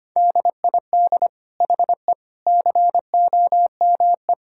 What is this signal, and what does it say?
Morse code, heard as a keyed tone, about 25 wpm, 700 Hz. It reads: DID HE COME
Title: Sentences from Top 1000 Words - Review Part 5 - 25wpm